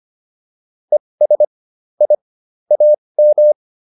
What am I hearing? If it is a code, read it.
Morse code, heard as a keyed tone, about 25 wpm, spaced as for 20 wpm, 600 Hz. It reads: ES I AM
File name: PR103 20